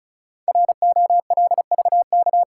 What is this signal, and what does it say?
Morse code, heard as a keyed tone, about 36 wpm, 700 Hz. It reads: ROLVK
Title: Code Groups - 5 Letter:Numbers - 35wpm